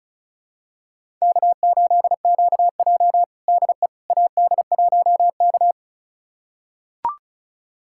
Morse code, heard as a keyed tone, about 34 wpm, 700 Hz. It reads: K8QJ DE AD1K
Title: USA Callsign DE Callsign - 4 Characters - 35wpm